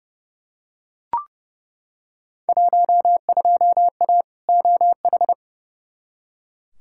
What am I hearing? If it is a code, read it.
Morse code, heard as a keyed tone, about 31 wpm, 700 Hz. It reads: 12A OH
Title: Field Day Exchanges Encore - 30wpm